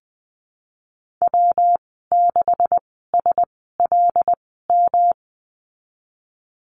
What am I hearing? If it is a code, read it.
Morse code, heard as a keyed tone, about 19 wpm, spaced as for 14 wpm, 700 Hz. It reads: W6SLM
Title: 128 Lesson 21g - Number 6 - Call Signs 20wpm